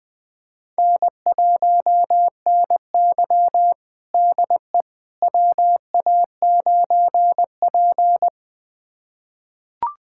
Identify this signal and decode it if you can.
Morse code, heard as a keyed tone, about 21 wpm, 700 Hz. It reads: N1NY DE WA9P
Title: USA Callsign DE Callsign - 4 Characters - 20wpm